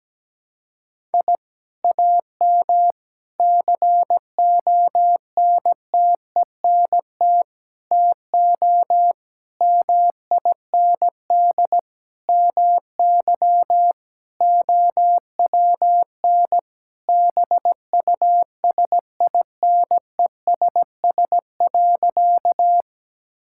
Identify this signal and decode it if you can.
Morse code, heard as a keyed tone, about 17 wpm, 700 Hz. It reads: I AM CONTENT TO MIND MY OWN BUSINESS.
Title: Sentences from Top 1000 Words - Review Part 6 - 17wpm